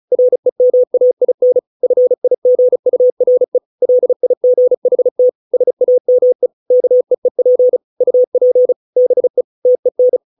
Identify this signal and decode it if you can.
Morse code, heard as a keyed tone, about 34 wpm, 500 Hz. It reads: REMAIN FIGURE LIGHT SAME KEEP UP BE TEN